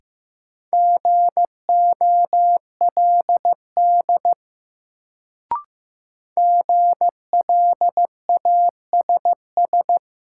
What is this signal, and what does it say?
Morse code, heard as a keyed tone, about 15 wpm, 700 Hz. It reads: GOLD GLASS
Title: Sentences from Top 1000 Words - Review Part 3 - New Word Drill - 15wpm